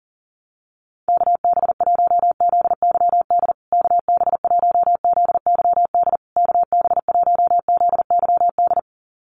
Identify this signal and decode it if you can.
Morse code, heard as a keyed tone, about 40 wpm, 700 Hz. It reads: KB1ZYD KB1ZYD KB1ZYD
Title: USA Call Signs Repeated 3x 40wpm